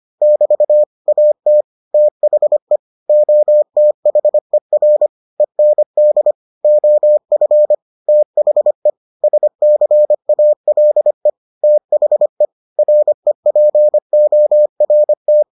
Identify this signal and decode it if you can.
Morse code, heard as a keyed tone, about 25 wpm, 600 Hz. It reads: <BT> AT THE OTHER END OF THE SCALE THE REPORT